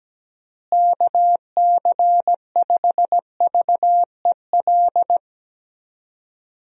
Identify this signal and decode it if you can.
Morse code, heard as a keyed tone, about 17 wpm, 700 Hz. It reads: KC5VEL
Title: USA Call Signs - 6 Characters - Repeated 3x - 17wpm